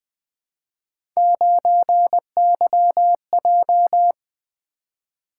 Morse code, heard as a keyed tone, about 20 wpm, 700 Hz. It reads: 9YJ
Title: Three Letter-Number Combinations - 20wpm